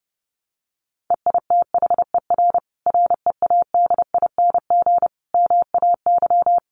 Morse code, heard as a keyed tone, about 28 wpm, 700 Hz. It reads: EITHER READING MAY
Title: Sentences from Top 1400 Words 30wpm